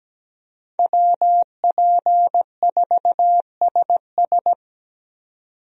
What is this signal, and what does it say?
Morse code, heard as a keyed tone, about 17 wpm, 700 Hz. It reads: WP4SS
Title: USA Call Signs - 5 Characters - 17wpm